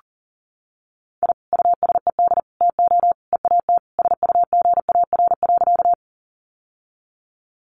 Morse code, heard as a keyed tone, about 40 wpm, 700 Hz. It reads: I USED TO EAT SUGAR.
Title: Sentences from Top 1200 Words 40wpm